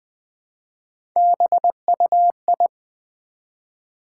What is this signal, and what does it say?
Morse code, heard as a keyed tone, about 20 wpm, 700 Hz. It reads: BUI
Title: Build A Word - 1 Letter At A Time - 5-Letter Words - 20wpm